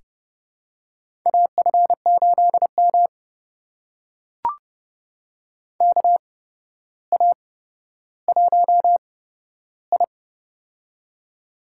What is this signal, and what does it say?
Morse code, heard as a keyed tone, about 30 wpm, spaced as for 13 wpm, 700 Hz. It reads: AF8M KA1I
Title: USA Call Signs - 4 Characters - 8x Character Spacing - 30wpm